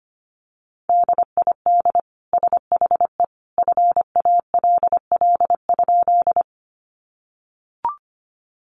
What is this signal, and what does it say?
Morse code, heard as a keyed tone, about 25 wpm, 700 Hz. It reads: DID SHE FALL?
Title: Sentences from Top 500 Words 25wpm